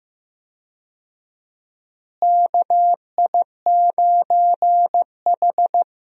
Morse code, heard as a keyed tone, about 15 wpm, 700 Hz. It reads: KI9H
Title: USA Call Signs - 4 Characters - Repeated 2x - 15wpm